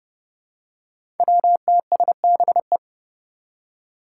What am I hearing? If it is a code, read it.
Morse code, heard as a keyed tone, about 30 wpm, 700 Hz. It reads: WTSBE
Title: Five Challenging Letter-Number Combinations 30wpm